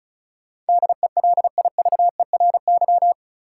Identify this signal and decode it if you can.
Morse code, heard as a keyed tone, about 36 wpm, 700 Hz. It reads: DELIVERY
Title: Sentences from Top 1500 Words - New Word Drill Alt - 35wpm